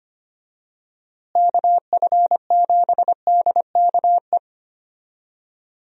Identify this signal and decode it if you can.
Morse code, heard as a keyed tone, about 25 wpm, 700 Hz. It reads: KF7DKE
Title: USA Call Signs - 6 Characters - Repeated 3x - 25wpm